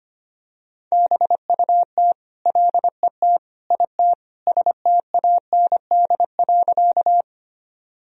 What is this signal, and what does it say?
Morse code, heard as a keyed tone, about 25 wpm, 700 Hz. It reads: BUT LET IT STAND.
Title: Sentences from Top 1000 Words - Review Part 1 - 25wpm